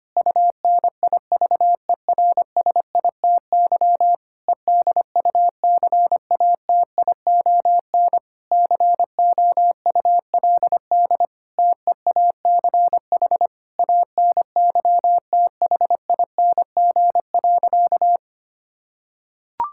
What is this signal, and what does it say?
Morse code, heard as a keyed tone, about 25 wpm, 700 Hz. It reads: UNIVERSITY EDUCATION COULD TEACH ANYTHING.
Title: Sentences from Top 1000 Words - Review Part 6 - 25wpm